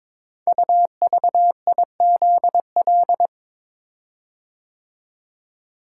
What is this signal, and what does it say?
Morse code, heard as a keyed tone, about 22 wpm, 700 Hz. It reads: UVIZL
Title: Code Groups - 5 Letters - 22wpm